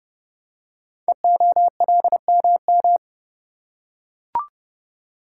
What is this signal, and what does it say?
Morse code, heard as a keyed tone, about 30 wpm, 700 Hz. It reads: EOLMM
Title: Code Groups - 5 Letters - 30wpm